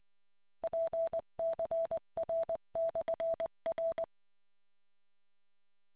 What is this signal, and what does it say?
Morse code, heard as a keyed tone, about 20 wpm, 660 Hz. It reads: PCR/R